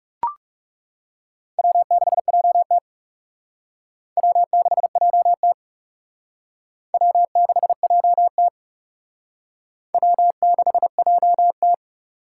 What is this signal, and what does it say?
Morse code, heard as a keyed tone, about 37 wpm, 700 Hz. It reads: W6JT W6JT W6JT W6JT
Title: Speed Racing - USA Call Signs 30wpm